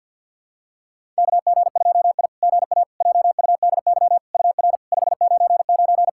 Morse code, heard as a keyed tone, about 50 wpm, 700 Hz. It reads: KG2I GA JUDY UR 599
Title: SOTA QSO Practice 50wpm